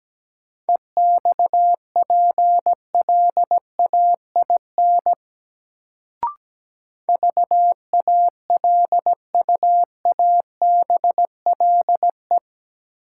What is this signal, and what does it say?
Morse code, heard as a keyed tone, about 17 wpm, 700 Hz. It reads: EXPLAIN VALUABLE
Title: Sentences from Top 1100 Words - New Word Drill - 17wpm